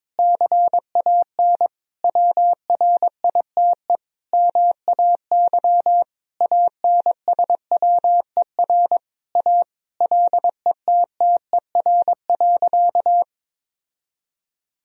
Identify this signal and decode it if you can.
Morse code, heard as a keyed tone, about 21 wpm, 700 Hz. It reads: CAN WRITE MAY ANSWER A LETTER.
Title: Sentences from Top 1000 Words - Review Part 1 - 22wpm